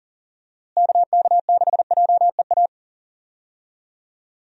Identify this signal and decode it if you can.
Morse code, heard as a keyed tone, about 40 wpm, 700 Hz. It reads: KK6JEA
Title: USA Call Signs - 6 Characters - Repeated 2x - 40wpm